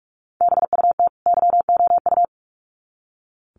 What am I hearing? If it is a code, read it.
Morse code, heard as a keyed tone, about 44 wpm, 700 Hz. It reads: BUT YOU
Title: Sets of 2 Words 45wpm